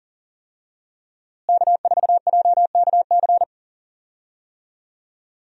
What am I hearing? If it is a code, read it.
Morse code, heard as a keyed tone, about 37 wpm, 700 Hz. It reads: K4JKC
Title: USA Call Signs - 5 Characters - Repeated 3x - 40wpm